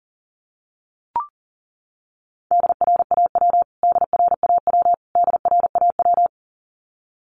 Morse code, heard as a keyed tone, about 37 wpm, 700 Hz. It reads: DRAW DRAW DRAW
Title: Top 500 Words Repeated 40wpm